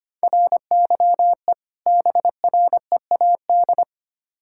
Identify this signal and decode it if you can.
Morse code, heard as a keyed tone, about 25 wpm, 700 Hz. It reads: RYE BREAD
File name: practice / 25wpm / Difficult Words with Hints - 25wpm